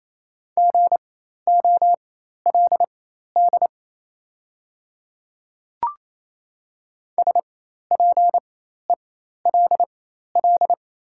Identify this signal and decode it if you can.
Morse code, heard as a keyed tone, about 28 wpm, spaced as for 13 wpm, 700 Hz. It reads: GOLD SPELL
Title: 4x Character Spacing - Top 500 Words - 28wpm